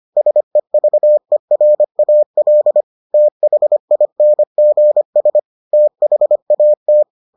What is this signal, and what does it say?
Morse code, heard as a keyed tone, about 25 wpm, 600 Hz. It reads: SEVERAL THINGS THAT